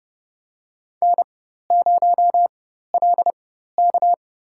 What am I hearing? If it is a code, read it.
Morse code, heard as a keyed tone, about 30 wpm, spaced as for 14 wpm, 700 Hz. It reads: N0LK
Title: USA Call Signs - 4 Characters - 4x Character Spacing - 30wpm